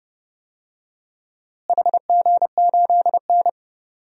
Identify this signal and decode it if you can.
Morse code, heard as a keyed tone, about 31 wpm, 700 Hz. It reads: HG8N
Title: DXCC-100 Call Signs - 4 Characters - Repeated 2x - 30wpm